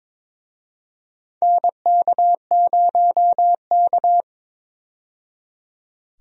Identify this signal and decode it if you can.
Morse code, heard as a keyed tone, about 21 wpm, 700 Hz. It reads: NK0K
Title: USA Call Signs - 4 Characters - 22wpm